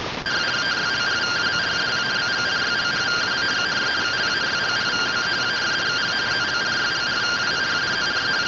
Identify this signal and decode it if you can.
Two-tone teleprinter signal, about 170 Hz shift.